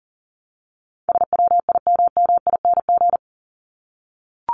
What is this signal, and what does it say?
Morse code, heard as a keyed tone, about 40 wpm, 700 Hz. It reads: SWIMMING
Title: Top 1501-1750 Words 40wpm